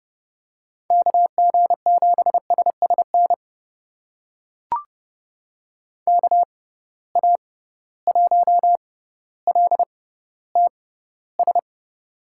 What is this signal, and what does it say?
Morse code, heard as a keyed tone, about 31 wpm, spaced as for 15 wpm, 700 Hz. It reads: KG7SSN KA1LTS
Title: USA Call Signs - 6x Character Spacing - 30wpm